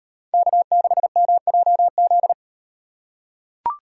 Morse code, heard as a keyed tone, about 38 wpm, 700 Hz. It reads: K6MJZ